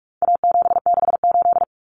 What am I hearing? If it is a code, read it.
Morse code, heard as a keyed tone, about 44 wpm, 700 Hz. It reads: A768
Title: Common Cut Numbers - 4-Digit 45wpm